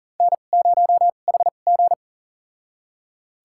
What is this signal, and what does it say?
Morse code, heard as a keyed tone, about 40 wpm, spaced as for 29 wpm, 700 Hz. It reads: N0HG
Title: USA Call Signs - 4 Characters - 2x Character Spacing - 40wpm